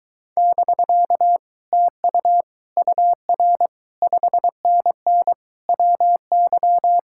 Morse code, heard as a keyed tone, about 22 wpm, 700 Hz. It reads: <BK> TU UR 5NN WY